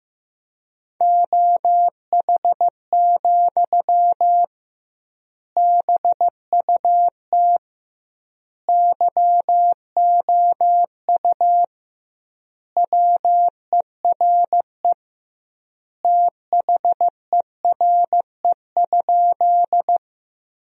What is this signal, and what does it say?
Morse code, heard as a keyed tone, about 16 wpm, 700 Hz. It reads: OH, BUT YOU WERE THERE?
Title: 2x Word-Spacing - Sentences from Top 500 Words - 15wpm